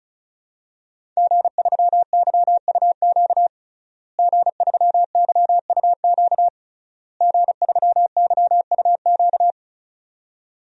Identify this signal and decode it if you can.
Morse code, heard as a keyed tone, about 36 wpm, 700 Hz. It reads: G3YUQ G3YUQ G3YUQ
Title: DXCC-100 Call Signs - 5 Characters - Repeated 3x - 35wpm